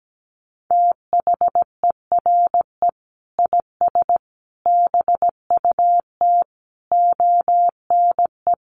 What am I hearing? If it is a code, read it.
Morse code, heard as a keyed tone, about 17 wpm, 700 Hz. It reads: THERE IS BUT ONE
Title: Sentences from Top 100 Words 17wpm